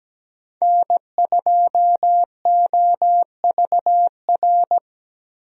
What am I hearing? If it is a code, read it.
Morse code, heard as a keyed tone, about 17 wpm, 700 Hz. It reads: N2OVR
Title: USA Call Signs - 5 Characters - Repeated 3x - 17wpm